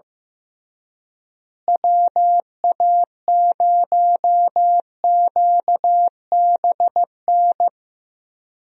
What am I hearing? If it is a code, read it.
Morse code, heard as a keyed tone, about 15 wpm, 700 Hz. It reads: WA0QBN
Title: USA Call Signs - 6 Characters - Repeated 3x - 15wpm